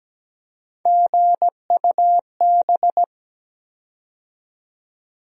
Morse code, heard as a keyed tone, about 17 wpm, 700 Hz. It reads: GUB
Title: USA Call Sign Postfixes - 3 Letters 17wpm